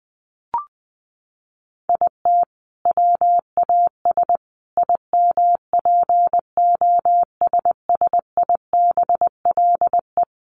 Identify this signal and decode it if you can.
Morse code, heard as a keyed tone, about 20 wpm, 700 Hz. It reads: IT WAS IMPOSSIBLE
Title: Sentences from Top 1100 Words 20wpm